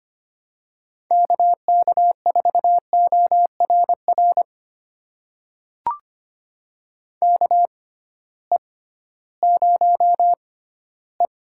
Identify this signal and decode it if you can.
Morse code, heard as a keyed tone, about 25 wpm, spaced as for 15 wpm, 700 Hz. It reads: KK4ORR KE0E
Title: USA Call Signs - 6 Characters - 6x Character Spacing - 25wpm